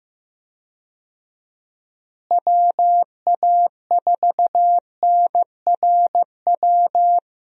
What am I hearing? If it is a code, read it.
Morse code, heard as a keyed tone, about 15 wpm, 700 Hz. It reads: WA4NRW
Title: USA Call Signs - 6 Characters - Repeated 3x - 15wpm